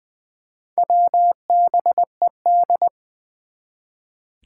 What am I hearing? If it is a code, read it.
Morse code, heard as a keyed tone, about 20 wpm, 700 Hz. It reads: WBED
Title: Four Challenging Letter-Number Combinations 20wpm